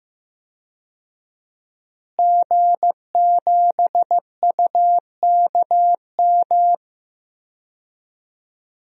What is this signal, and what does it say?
Morse code, heard as a keyed tone, about 16 wpm, 700 Hz. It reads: G7UKM
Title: DXCC-100 Call Signs - 5 Characters - Repeated 2x - 15wpm